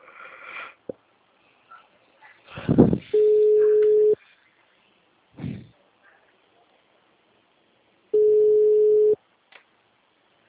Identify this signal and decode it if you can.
Ringback tone (Europe, 425 Hz), with speech.
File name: out- -2007- - - .74214